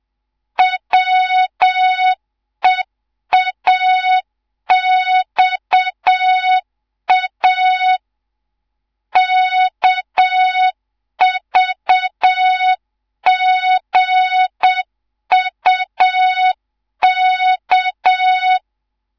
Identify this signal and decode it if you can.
Morse code, heard as a keyed tone, about 7 wpm, 3700 Hz, with noise. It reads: WEAXA KVGUK
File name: Morze-HA2MN / hang / 11 GYAKORLAT